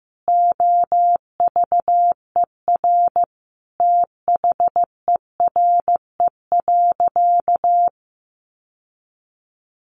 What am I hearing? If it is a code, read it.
Morse code, heard as a keyed tone, about 15 wpm, 700 Hz. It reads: OVER THERE.
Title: Sentences from Top 1000 Words 15wpm